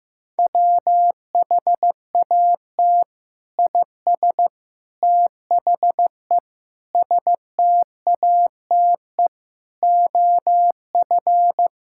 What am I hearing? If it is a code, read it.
Morse code, heard as a keyed tone, about 15 wpm, 700 Hz. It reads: WHAT IS THE STATE OF
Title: Sentences from Top 1000 Words - Review Part 5 - 15wpm